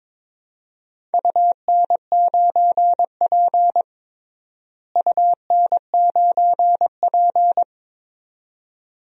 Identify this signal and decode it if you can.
Morse code, heard as a keyed tone, about 22 wpm, 700 Hz. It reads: UN9P UN9P
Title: DXCC-100 Call Signs - 4 Characters - Repeated 2x - 22wpm